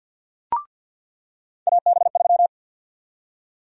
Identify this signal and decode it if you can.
Morse code, heard as a keyed tone, about 53 wpm, 700 Hz. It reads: AB3
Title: USA Call Sign Prefixes - 50wpm